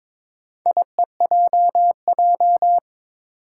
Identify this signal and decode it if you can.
Morse code, heard as a keyed tone, about 22 wpm, 700 Hz. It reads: IEJJ
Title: Four Challenging Letter-Number Combinations 22wpm